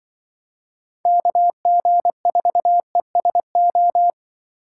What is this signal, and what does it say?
Morse code, heard as a keyed tone, about 24 wpm, 700 Hz. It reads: KG4ESO
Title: Speed Racing - USA 2x3 Call Signs 20wpm